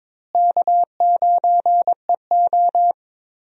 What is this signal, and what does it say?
Morse code, heard as a keyed tone, about 22 wpm, 700 Hz. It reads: K9EO
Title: USA Call Signs - 4 Characters - Repeated 3x - 22wpm